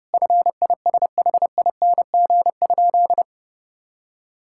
Morse code, heard as a keyed tone, about 30 wpm, 700 Hz. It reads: FISHING?
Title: Sentences from Top 1000 Words - Review Part 5 - 30wpm